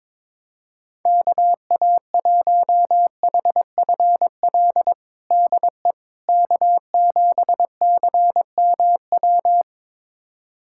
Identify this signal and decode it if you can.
Morse code, heard as a keyed tone, about 22 wpm, 700 Hz. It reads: KA1HFL DE K7CMW
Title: USA Callsign DE Callsign 22wpm